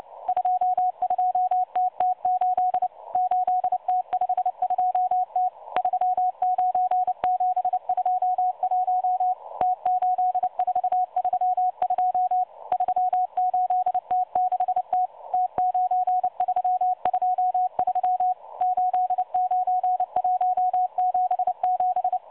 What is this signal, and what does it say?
Morse code, heard as a keyed tone, about 29 wpm, 730 Hz. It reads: 22TT8 8T52T 39721 T8432 38T6T T9323 89177
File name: M12 - 8116 kHz CW 2-11-14